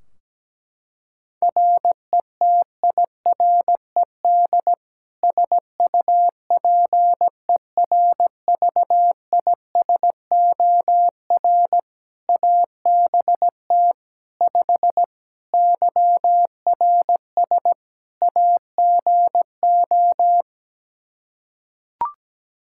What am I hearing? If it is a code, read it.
Morse code, heard as a keyed tone, about 17 wpm, 700 Hz. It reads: RETIRED SUPERVISOR ABT 5 YRS AGO
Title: Practice QSO - Exchange 3 - More Info - 17wpm